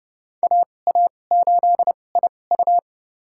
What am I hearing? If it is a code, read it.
Morse code, heard as a keyed tone, about 29 wpm, spaced as for 21 wpm, 700 Hz. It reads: AA8IU
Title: USA Call Signs - 2x Character Spacing - 30wpm